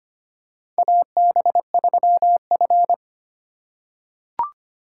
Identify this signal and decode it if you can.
Morse code, heard as a keyed tone, about 25 wpm, 700 Hz. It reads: AB3F
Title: USA Call Signs - 4 Characters - 3x Character Spacing - 25wpm